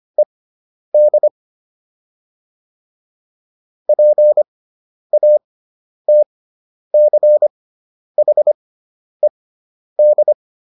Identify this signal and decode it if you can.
Morse code, heard as a keyed tone, about 25 wpm, spaced as for 10 wpm, 600 Hz. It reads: ED PATCHED